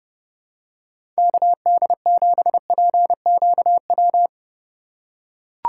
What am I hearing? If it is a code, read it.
Morse code, heard as a keyed tone, about 30 wpm, 700 Hz. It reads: KD7PQW
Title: USA Call Signs - 6x Character Spacing - 30wpm